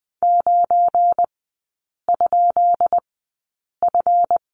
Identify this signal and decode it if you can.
Morse code, heard as a keyed tone, about 20 wpm, spaced as for 8 wpm, 700 Hz. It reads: 9?F
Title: 172 Lesson 28e - Number 9 - Three Characters 20wpm